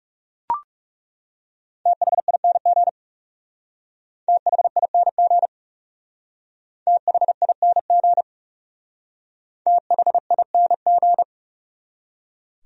Morse code, heard as a keyed tone, about 40 wpm, 700 Hz. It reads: THING THING THING THING